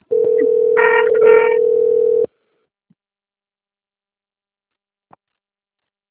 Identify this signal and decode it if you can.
Ringback tone (North America), 440 + 480 Hz.